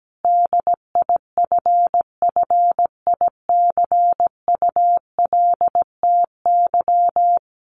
Morse code, heard as a keyed tone, about 17 wpm, 700 Hz. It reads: DIFFICULTY
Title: Sentences from Top 1500 Words 17wpm